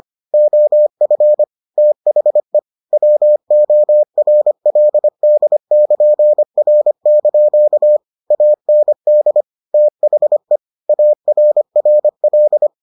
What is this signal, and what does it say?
Morse code, heard as a keyed tone, about 25 wpm, 600 Hz. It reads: OF THE WORLD<KN>R) AND THE ARRL